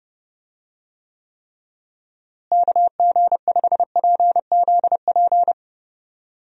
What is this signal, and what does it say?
Morse code, heard as a keyed tone, about 29 wpm, 700 Hz. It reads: KG5PZP
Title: USA Call Signs - 6 Characters - Repeated 3x - 30wpm